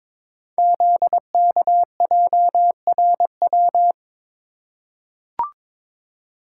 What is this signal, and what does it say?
Morse code, heard as a keyed tone, about 22 wpm, 700 Hz. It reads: ZKJRW
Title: Code Groups - 5 Letter:Numbers - 22wpm